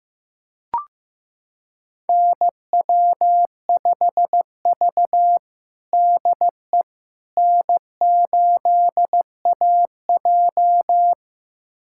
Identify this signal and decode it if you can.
Morse code, heard as a keyed tone, about 15 wpm, 700 Hz. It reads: NW5V DE N8AJ
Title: USA Callsign DE Callsign - 4 Characters - 15wpm